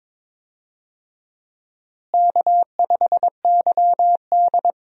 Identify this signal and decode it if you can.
Morse code, heard as a keyed tone, about 22 wpm, 700 Hz. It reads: K5YD
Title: USA Call Signs - 4 Characters - Repeated 3x - 22wpm